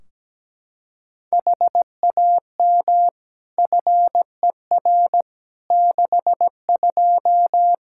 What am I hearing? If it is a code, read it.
Morse code, heard as a keyed tone, about 17 wpm, 700 Hz. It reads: HAM FER 62
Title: Practice QSO - Exchange 3 - More Info - 17wpm